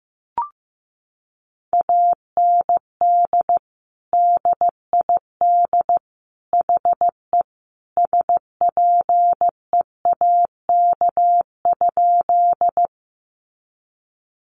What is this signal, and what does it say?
Morse code, heard as a keyed tone, about 14 wpm, 700 Hz. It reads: AND DID HE SPEAK?
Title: Sentences from Top 1100 Words 15wpm